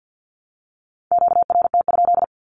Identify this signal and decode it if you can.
Morse code, heard as a keyed tone, about 50 wpm, 700 Hz. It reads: QRT?
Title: Q-Codes Repeated 3x 50wpm